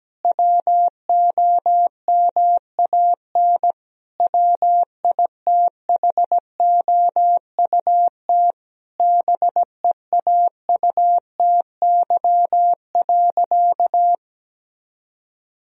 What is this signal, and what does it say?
Morse code, heard as a keyed tone, about 17 wpm, 700 Hz. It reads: WOMAN WITHOUT BEAUTY.
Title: Sentences from Top 1000 Words - Review Part 3 - 17wpm